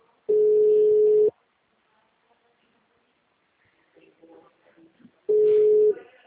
Ringback tone (Europe, 425 Hz).